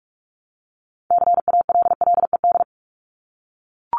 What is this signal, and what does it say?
Morse code, heard as a keyed tone, about 44 wpm, 700 Hz. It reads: CALLED
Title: ED Suffix - Top 100 Words with Base Words 45wpm